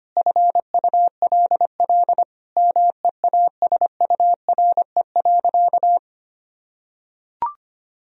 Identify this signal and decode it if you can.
Morse code, heard as a keyed tone, about 25 wpm, 700 Hz. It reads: FULL MEASURE.
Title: Sentences from Top 1000 Words - Review Part 3 - 25wpm